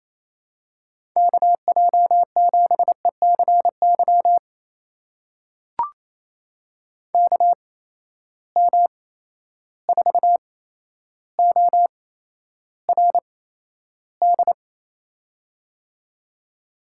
Morse code, heard as a keyed tone, about 28 wpm, spaced as for 12 wpm, 700 Hz. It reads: KJ7ECY KM4ORD
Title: USA Call Signs - 6 Characters - 8x Character Spacing - 28wpm